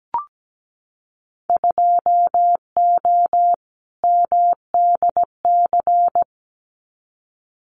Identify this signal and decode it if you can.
Morse code, heard as a keyed tone, about 17 wpm, 700 Hz. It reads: 2O MDC
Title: Winter Field Day Exchanges 17wpm